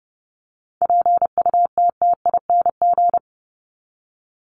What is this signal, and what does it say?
Morse code, heard as a keyed tone, about 30 wpm, 700 Hz. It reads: PUTTING
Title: ING Suffix - Top 100 Words with Base Words 30wpm